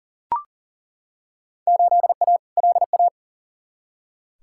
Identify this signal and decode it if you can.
Morse code, heard as a keyed tone, about 40 wpm, 700 Hz. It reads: 8A LA
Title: Field Day Exchanges Encore - 40wpm